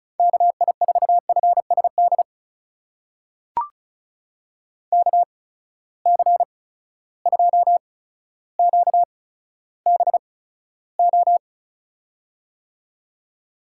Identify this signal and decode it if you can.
Morse code, heard as a keyed tone, about 36 wpm, spaced as for 15 wpm, 700 Hz. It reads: KI4FSD KC2QBO
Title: USA Call Signs - 6 Characters - 8x Character Spacing - 35wpm